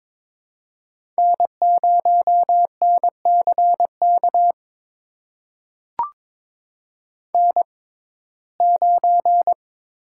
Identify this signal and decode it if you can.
Morse code, heard as a keyed tone, about 23 wpm, spaced as for 16 wpm, 700 Hz. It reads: N0NCK N9
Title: USA Call Signs - 5 Characters - 6x Character Spacing - 22wpm